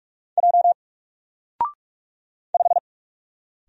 Morse code, heard as a keyed tone, about 44 wpm, 700 Hz. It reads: J 5
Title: Single Letter-Number Alt 45wpm